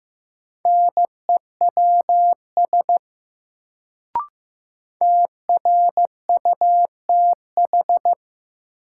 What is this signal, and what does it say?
Morse code, heard as a keyed tone, about 16 wpm, 700 Hz. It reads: NEWS TRUTH
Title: Sentences from Top 800 Words - New Word Drill - 15wpm